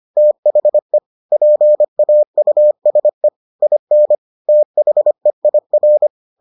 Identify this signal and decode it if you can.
Morse code, heard as a keyed tone, about 25 wpm, 600 Hz. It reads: THE PAUSE IN THEIR